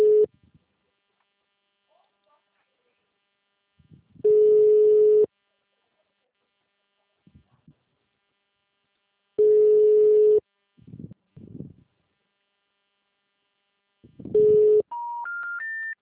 Special information tone.